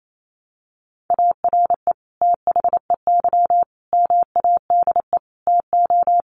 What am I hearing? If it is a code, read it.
Morse code, heard as a keyed tone, about 28 wpm, 700 Hz. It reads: ARE THEY MADE TO
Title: Sentences from Top 400 Words 28wpm